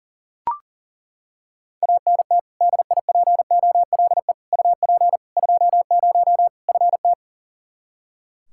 Morse code, heard as a keyed tone, about 40 wpm, 700 Hz. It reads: ANT DIPOLE UP 20 FT
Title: Practice QSO - Exchange 2 - Info - 40wpm